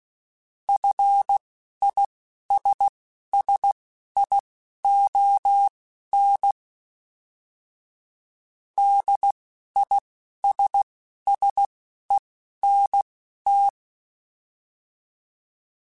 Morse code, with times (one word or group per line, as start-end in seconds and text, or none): 0.7-6.5 FISSION
8.8-13.7 DISSENT